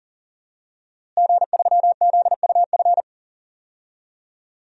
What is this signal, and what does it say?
Morse code, heard as a keyed tone, about 40 wpm, 700 Hz. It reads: G3ZUF
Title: DXCC-100 Call Signs - 5 Characters - Repeated 2x - 40wpm